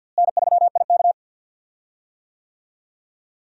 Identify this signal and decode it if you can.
Morse code, heard as a keyed tone, about 46 wpm, 700 Hz. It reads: N3IK
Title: USA Call Signs - 4 Characters - Repeated 3x - 50wpm